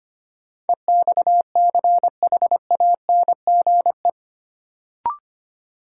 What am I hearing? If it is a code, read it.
Morse code, heard as a keyed tone, about 25 wpm, 700 Hz. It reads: EXCHANGE
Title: Sentences from Top 1000 Words - Review Part 6 - New Word Drill - 25wpm